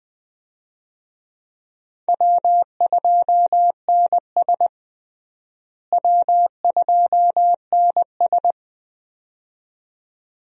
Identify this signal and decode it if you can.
Morse code, heard as a keyed tone, about 20 wpm, 700 Hz. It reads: W2NS W2NS
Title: USA Call Signs - 4 Characters - Repeated 2x - 20wpm